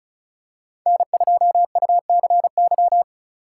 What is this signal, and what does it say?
Morse code, heard as a keyed tone, about 36 wpm, 700 Hz. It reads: N2UCY
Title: USA Call Signs - 2x Character Spacing - 35wpm